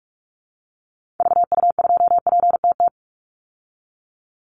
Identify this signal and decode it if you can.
Morse code, heard as a keyed tone, about 44 wpm, 700 Hz. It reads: VU2PTT
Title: Call Signs Repeated 3x - 45wpm